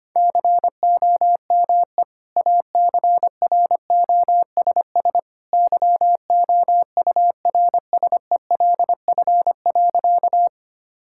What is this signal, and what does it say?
Morse code, heard as a keyed tone, about 25 wpm, 700 Hz. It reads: COME ACROSS YOURSELF.
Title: Sentences from Top 1000 Words - Review Part 5 - 25wpm